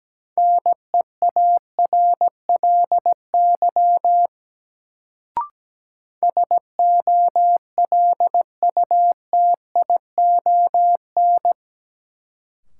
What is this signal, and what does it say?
Morse code, heard as a keyed tone, about 17 wpm, 700 Hz. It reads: NEARLY SOLUTION
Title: Sentences from Top 1300 Words - New Word Drill - 17wpm